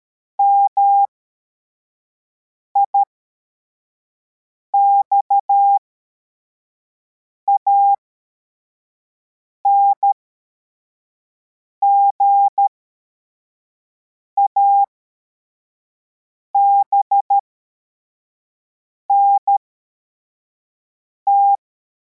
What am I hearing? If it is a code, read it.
Morse code, heard as a keyed tone, about 13 wpm, spaced as for 4 wpm, 800 Hz. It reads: MIXAN GABNT
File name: formation / cw / Text44